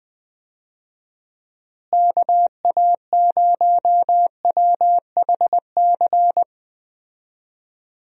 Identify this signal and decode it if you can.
Morse code, heard as a keyed tone, about 20 wpm, 700 Hz. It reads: KA0WHC
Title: USA Call Signs - 6 Characters - Repeated 2x - 20wpm